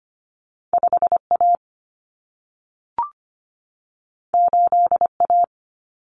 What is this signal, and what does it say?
Morse code, heard as a keyed tone, about 25 wpm, 700 Hz. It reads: 5A 8A
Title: Common Cut Numbers - 2-Digit 25wpm